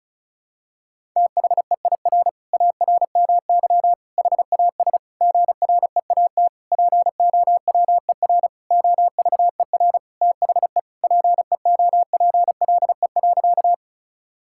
Morse code, heard as a keyed tone, about 36 wpm, 700 Hz. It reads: THEIR ARMY HAS GREAT POWER OVER THE PEOPLE.
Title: Sentences from Top 1000 Words - Review Part 5 - 35wpm